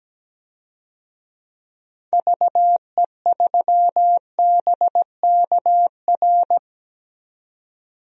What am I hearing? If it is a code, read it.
Morse code, heard as a keyed tone, about 17 wpm, 700 Hz. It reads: VE3BKR
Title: DXCC-100 Call Signs - 6 Characters - 17wpm